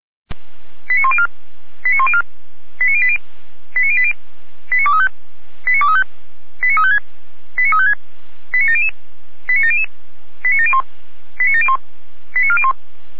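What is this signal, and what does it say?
Five-tone selective call (ZVEI), heard as quick runs of five tones, tones of about 70 ms. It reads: 89104 89104 89080 89080 89235 89235 89356 89356 89700 89700 89701 89701 89401
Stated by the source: ZVEI five-tone selective call received on SDR.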